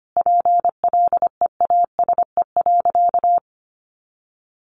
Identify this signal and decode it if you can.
Morse code, heard as a keyed tone, about 25 wpm, 700 Hz. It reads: PLEASE.
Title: Sentences from Top 1500 Words 25wpm